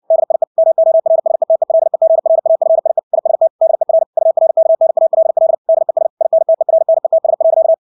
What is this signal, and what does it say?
Morse code, heard as a keyed tone, about 60 wpm, 650 Hz. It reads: DIE MORSETELEGRAFIE IST DER URKNALL DES INTERNETS.
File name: dk5ke / Motto 300BpM